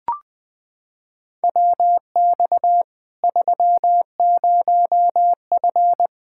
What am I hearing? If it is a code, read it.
Morse code, heard as a keyed tone, about 21 wpm, 700 Hz. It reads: WX 30F
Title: Practice QSO - Exchange 2 - Info - 20wpm